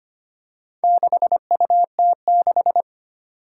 Morse code, heard as a keyed tone, about 25 wpm, 700 Hz. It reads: 6UT6
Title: Four Challenging Letter-Number Combinations 25wpm